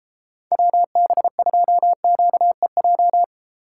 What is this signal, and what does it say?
Morse code, heard as a keyed tone, about 33 wpm, 700 Hz. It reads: WB2QEJ